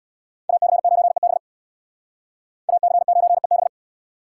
Morse code, heard as a keyed tone, about 61 wpm, 700 Hz. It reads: AC9EB AC9EB
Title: Speed Racing - USA Call Signs 50wpm